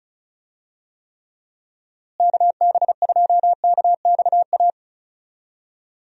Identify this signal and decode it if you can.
Morse code, heard as a keyed tone, about 34 wpm, 700 Hz. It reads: KB2KXA
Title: USA Call Signs - 6 Characters - Repeated 3x - 35wpm